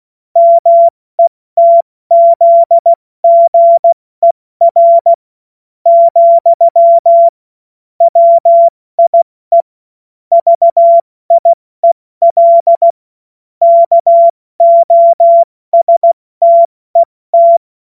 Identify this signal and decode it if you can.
Morse code, heard as a keyed tone, about 16 wpm, spaced as for 14 wpm, 680 Hz. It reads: METZGER, WIE VIEL KOSTET
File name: Morsekurs / Witze / Witz 07. Juni 2022 70 F80